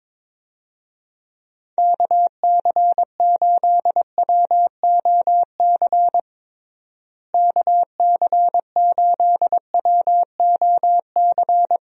Morse code, heard as a keyed tone, about 22 wpm, 700 Hz. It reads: KC8WOC KC8WOC
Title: USA Call Signs - 6 Characters - Repeated 2x - 22wpm